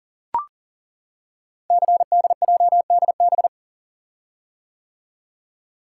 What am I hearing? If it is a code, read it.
Morse code, heard as a keyed tone, about 40 wpm, 700 Hz. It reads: CDJDB
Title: Code Groups - 5 Letters - 40wpm